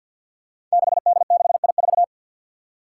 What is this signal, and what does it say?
Morse code, heard as a keyed tone, about 52 wpm, 700 Hz. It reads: 6D6I4
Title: Five Challenging Letter-Number Combinations 50wpm